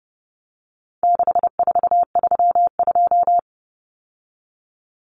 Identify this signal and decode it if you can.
Morse code, heard as a keyed tone, about 30 wpm, 700 Hz. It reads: 6432
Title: Common Cut Numbers - 4-Digit 30wpm